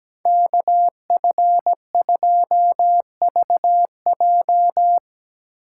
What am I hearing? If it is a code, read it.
Morse code, heard as a keyed tone, about 19 wpm, 700 Hz. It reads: KF2VJ
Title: Speed Racing - USA 2x2 Call Signs 17wpm